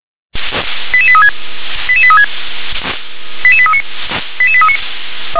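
Five-tone selective call (ZVEI), heard as quick runs of five tones, tones of about 70 ms. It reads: 88836 88836 88839 88839
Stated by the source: ZVEI five-tone selective call received by SDR.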